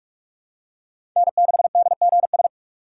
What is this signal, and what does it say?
Morse code, heard as a keyed tone, about 44 wpm, 700 Hz. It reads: N6DGS
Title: USA Call Signs - 5 Characters - 3x Character Spacing - 45wpm